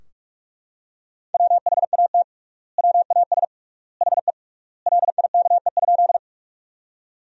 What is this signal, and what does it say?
Morse code, heard as a keyed tone, about 41 wpm, 700 Hz. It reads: WHAT WAS HE LIKE?
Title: 3x Word-Spacing - Sentences from Top 100 Words - 45wpm